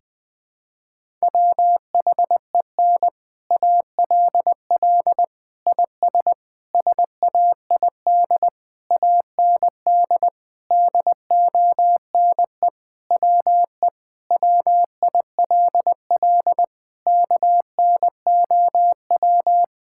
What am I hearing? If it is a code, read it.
Morse code, heard as a keyed tone, about 20 wpm, 700 Hz. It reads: WHEN ALL IS SAID AND DONE WE WILL KNOW
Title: Sentences with 6-Word Phrases - Spoken Beginning - 20wpm